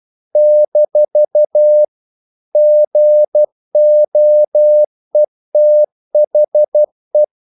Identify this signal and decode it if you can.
Morse code, heard as a keyed tone, about 12 wpm, 600 Hz. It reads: - GOETHE
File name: CW / Practice / EnglishQuotations0312